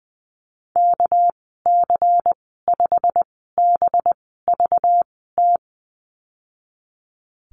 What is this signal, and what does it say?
Morse code, heard as a keyed tone, about 20 wpm, spaced as for 15 wpm, 700 Hz. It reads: KC5BVT
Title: 219 Lesson 35g - Number 0 - Call Signs 20wpm